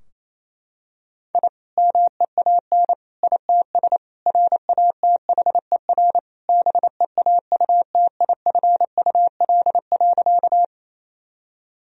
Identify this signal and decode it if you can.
Morse code, heard as a keyed tone, about 28 wpm, 700 Hz. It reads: I MEAN ITS RATHER BEAUTIFUL.
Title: Sentences from Top 1000 Words - Review Part 4 - 28wpm